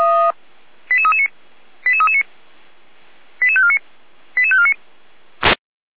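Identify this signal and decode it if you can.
Five-tone selective call (ZVEI), heard as quick runs of five tones, tones of about 70 ms. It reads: 88339 88339 88549 88549
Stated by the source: ZVEI five-tone selective call received on SDR.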